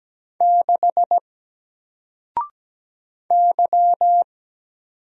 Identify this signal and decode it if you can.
Morse code, heard as a keyed tone, about 17 wpm, 700 Hz. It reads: 6 Y
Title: Single Letter-Number 17wpm